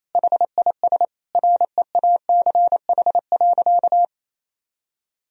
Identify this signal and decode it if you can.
Morse code, heard as a keyed tone, about 28 wpm, 700 Hz. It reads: HIS REACH.
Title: Sentences from Top 1000 Words - Review Part 4 - 28wpm